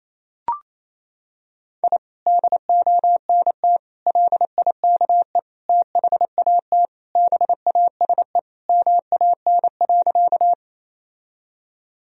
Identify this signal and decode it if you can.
Morse code, heard as a keyed tone, about 28 wpm, 700 Hz. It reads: I DONT LIKE THAT BASE MAN.
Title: Sentences from Top 1000 Words - Review Part 1 - 28wpm